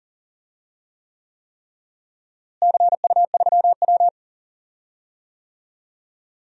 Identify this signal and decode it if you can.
Morse code, heard as a keyed tone, about 40 wpm, 700 Hz. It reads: CU3W
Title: DXCC-100 Call Signs - 4 Characters - Repeated 3x - 40wpm